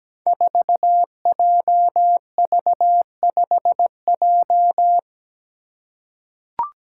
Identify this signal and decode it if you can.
Morse code, heard as a keyed tone, about 17 wpm, 700 Hz. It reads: 4JV5J
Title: Five Challenging Letter-Number Combinations 17wpm